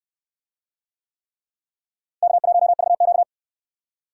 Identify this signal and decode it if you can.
Morse code, heard as a keyed tone, about 64 wpm, 700 Hz. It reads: K1VQ